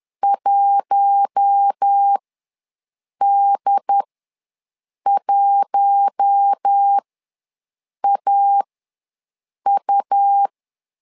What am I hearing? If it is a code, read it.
Morse code, heard as a keyed tone, about 10 wpm, spaced as for 6 wpm, 800 Hz. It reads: 1D1AU
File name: LEZIONI / 4 Q1 ws 10wpm 2538